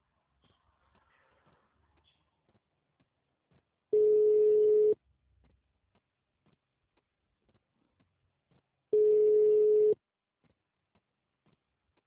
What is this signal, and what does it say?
Ringback tone (Europe, 425 Hz).
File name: out- -2042- - - .4529